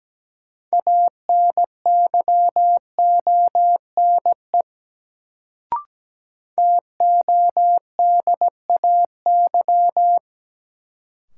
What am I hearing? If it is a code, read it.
Morse code, heard as a keyed tone, about 17 wpm, 700 Hz. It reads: ANYONE TODAY
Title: Sets of 5 Words Encore - New Word Drill - 17wpm